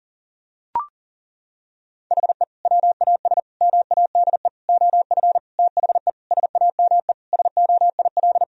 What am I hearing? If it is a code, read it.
Morse code, heard as a keyed tone, about 37 wpm, 700 Hz. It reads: HE WAS MADE OF THE SAME SOIL
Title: Sentences from Top 1000 Words - Review Part 4 - 40wpm